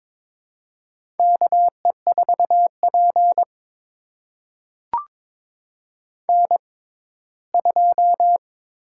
Morse code, heard as a keyed tone, about 22 wpm, spaced as for 15 wpm, 700 Hz. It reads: KE4P N2
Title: USA Call Signs - 4 Characters - 6x Character Spacing - 22wpm